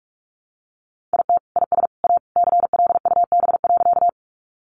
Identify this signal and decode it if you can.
Morse code, heard as a keyed tone, about 44 wpm, 700 Hz. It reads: IT IS A CLUB.
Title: Sentences from Top 1200 Words 45wpm